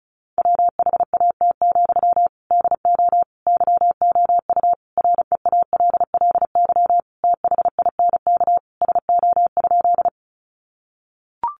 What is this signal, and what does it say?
Morse code, heard as a keyed tone, about 34 wpm, 700 Hz. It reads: WHAT, DO YOU REALLY THINK SO?
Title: Sentences from Top 600 Words 35wpm